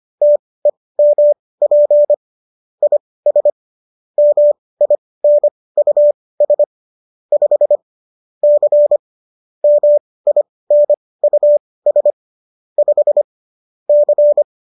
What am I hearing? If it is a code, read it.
Morse code, heard as a keyed tone, about 25 wpm, spaced as for 18 wpm, 590 Hz. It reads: TEMP IS MINUS 5 C MINUS 5 C